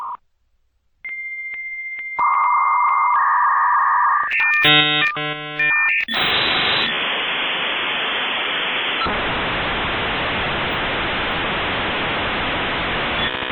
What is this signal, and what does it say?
Dial-up modem handshake.